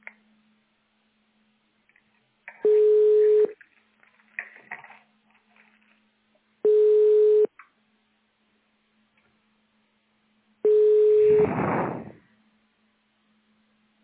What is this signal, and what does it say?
Ringback tone (425 Hz, 1 s on, 3 s off).
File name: 1040.3000.80aa3771-292e-490f-9e68